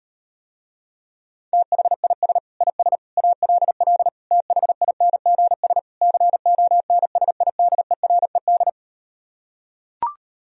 Morse code, heard as a keyed tone, about 38 wpm, 700 Hz. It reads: THIS IS ALL THINGS CONSIDERED